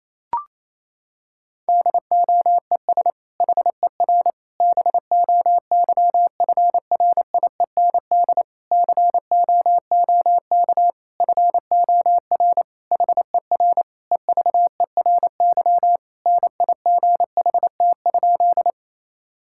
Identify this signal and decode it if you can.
Morse code, heard as a keyed tone, about 28 wpm, 700 Hz. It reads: DOES HER BOYFRIEND COOK FOR HER EVERY NIGHT?
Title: Sentences from Top 1000 Words - Review Part 5 - 28wpm